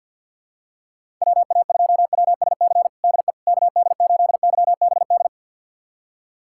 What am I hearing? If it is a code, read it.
Morse code, heard as a keyed tone, about 49 wpm, 690 Hz. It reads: WA2WSC DE KD8YBD
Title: USA Callsign DE Callsign - 6 Characters - 50wpm